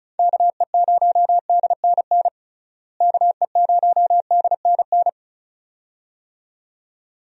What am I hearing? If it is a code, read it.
Morse code, heard as a keyed tone, about 36 wpm, 700 Hz. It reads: KE0DNN KE0DNN
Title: USA Call Signs - 6 Characters - Repeated 2x - 35wpm